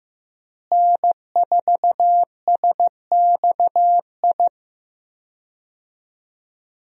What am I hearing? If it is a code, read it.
Morse code, heard as a keyed tone, about 15 wpm, 700 Hz. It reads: N4SXI
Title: USA Call Signs - 5 Characters - Repeated 3x - 15wpm